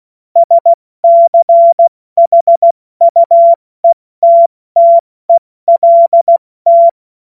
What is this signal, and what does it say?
Morse code, heard as a keyed tone, about 16 wpm, 680 Hz. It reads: SCHUETTELT
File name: Morsekurs / Witze / Witz 19. April 2022 70 F80